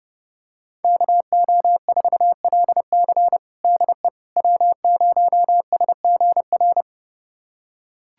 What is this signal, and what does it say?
Morse code, heard as a keyed tone, about 28 wpm, 700 Hz. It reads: KO4LC DE W0SGR
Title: USA Callsign DE Callsign - 5 Characters - 30wpm